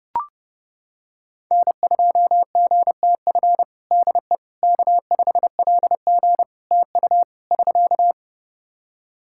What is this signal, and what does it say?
Morse code, heard as a keyed tone, about 29 wpm, 700 Hz. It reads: N2GTF DE K5LG TU <SK>
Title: Practice QSO - Exchange 4 - Ending - 30wpm